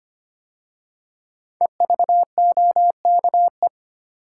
Morse code, heard as a keyed tone, about 25 wpm, 700 Hz. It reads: EVOKE
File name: 5 Letter Words - First 3 Letters Given - 25wpm